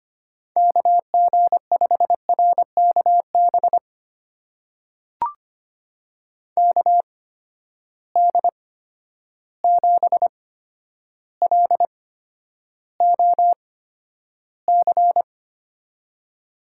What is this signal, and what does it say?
Morse code, heard as a keyed tone, about 25 wpm, spaced as for 11 wpm, 700 Hz. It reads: KG5RKB KD7LOC
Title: USA Call Signs - 6 Characters - 8x Character Spacing - 25wpm